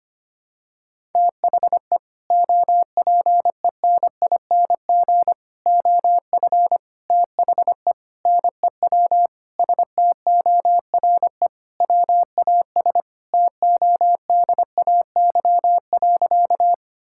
Morse code, heard as a keyed tone, about 25 wpm, 700 Hz. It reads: THE OPENING OF THE NEW STORE WAS TODAY.
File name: practice / 25wpm / Sentences with 3-Word Phrases - 25wpm